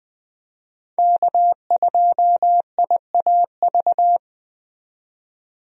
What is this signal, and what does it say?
Morse code, heard as a keyed tone, about 20 wpm, 700 Hz. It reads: K2IAV
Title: USA Call Signs - 5 Characters - 20wpm